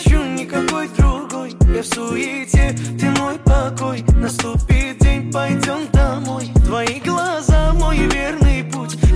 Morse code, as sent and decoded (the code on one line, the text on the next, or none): none